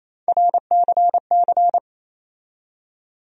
Morse code, heard as a keyed tone, about 28 wpm, 700 Hz. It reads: RCC
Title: USA Call Sign Postfixes - 3 Letters 28wpm